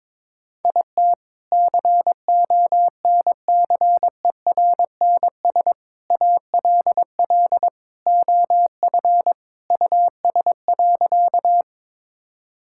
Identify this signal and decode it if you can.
Morse code, heard as a keyed tone, about 22 wpm, 700 Hz. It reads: IT CONCERNS ALL OF US.
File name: practice / 22wpm / Sentences with 3-Word Phrases - 22wpm